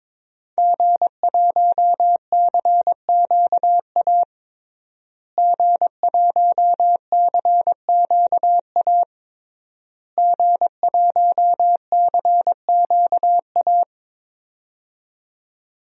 Morse code, heard as a keyed tone, about 22 wpm, 700 Hz. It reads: G1CQA G1CQA G1CQA
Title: DXCC-100 Call Signs - 5 Characters - Repeated 3x - 22wpm